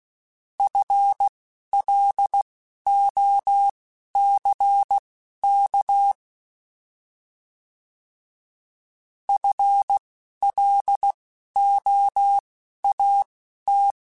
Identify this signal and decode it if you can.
Morse code, heard as a keyed tone, about 16 wpm, spaced as for 11 wpm, 800 Hz. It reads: FLOCK FLOAT
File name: Alphabetical / Alphabetical mp3s / Fs @10